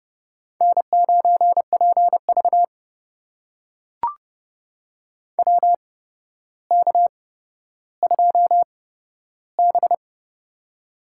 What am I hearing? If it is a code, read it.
Morse code, heard as a keyed tone, about 31 wpm, spaced as for 13 wpm, 700 Hz. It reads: N9PV WK2B
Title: USA Call Signs - 4 Characters - 8x Character Spacing - 30wpm